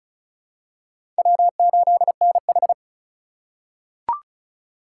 Morse code, heard as a keyed tone, about 36 wpm, 700 Hz. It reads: W8NH
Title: USA Call Signs - 3x Character Spacing - 35wpm